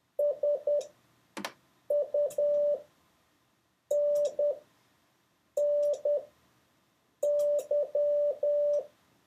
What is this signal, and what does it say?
Morse code, heard as a keyed tone, about 10 wpm, spaced as for 6 wpm, 580 Hz. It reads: SUNNY